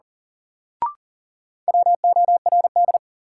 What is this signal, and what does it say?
Morse code, heard as a keyed tone, about 37 wpm, 700 Hz. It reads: WORD
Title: Sentences from Top 300 Words - New Word Drill - 40wpm